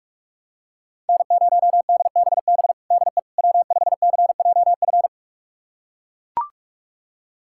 Morse code, heard as a keyed tone, about 42 wpm, 700 Hz. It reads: N0DBB DE W5CJF
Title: USA Callsign DE Callsign - 5 Characters - 45wpm